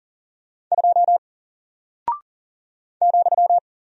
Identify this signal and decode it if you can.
Morse code, heard as a keyed tone, about 40 wpm, 700 Hz. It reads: 2 ,
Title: Single Character (Letter:Number:Punctuation:Prosign) Alt 40wpm